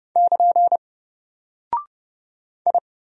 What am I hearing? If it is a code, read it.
Morse code, heard as a keyed tone, about 30 wpm, 700 Hz. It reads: <KN> I
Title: Single Character (Letter:Number:Punctuation:Prosign) Alt 30wpm